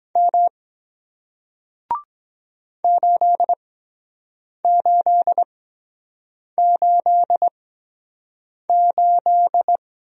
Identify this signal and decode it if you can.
Morse code, heard as a keyed tone, about 24 wpm, spaced as for 7 wpm, 700 Hz. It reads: M 8888